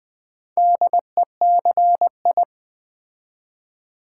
Morse code, heard as a keyed tone, about 20 wpm, 700 Hz. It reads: DECI
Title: Build A Word - 2 Letters At A Time - 8-Letter Words - 20wpm